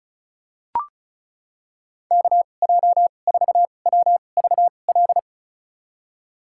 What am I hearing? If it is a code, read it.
Morse code, heard as a keyed tone, about 36 wpm, spaced as for 25 wpm, 700 Hz. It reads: KJ4WVL
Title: USA Call Signs - 6 Characters - 2x Character Spacing - 35wpm